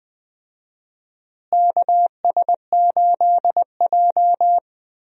Morse code, heard as a keyed tone, about 21 wpm, 700 Hz. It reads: KS8J
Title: USA Call Signs - 4 Characters - Repeated 3x - 20wpm